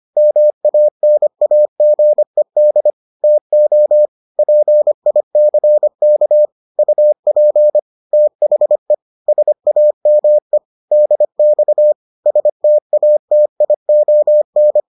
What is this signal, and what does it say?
Morse code, heard as a keyed tone, about 25 wpm, 600 Hz. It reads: MANAGED TO PICK UP THE SAME DX STATION